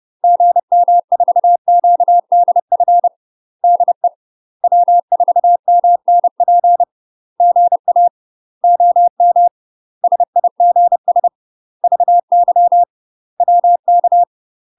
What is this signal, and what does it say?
Morse code, heard as a keyed tone, about 30 wpm, 700 Hz. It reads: GM4QDF DE W4MNP GA OM SIGS VY WK
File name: UK MASTER QSOs TRAINING / 30WPM / UK MASTER QSOs 0040